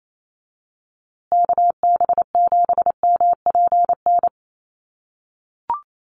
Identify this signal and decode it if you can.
Morse code, heard as a keyed tone, about 28 wpm, 700 Hz. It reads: KB7MPN
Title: USA Call Signs Repeated 3x 28wpm